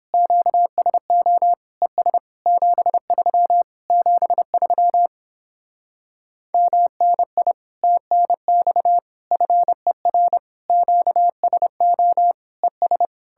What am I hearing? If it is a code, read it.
Morse code, heard as a keyed tone, about 29 wpm, 700 Hz. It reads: QSO ES 73 73 MNI TNX FER QSO ES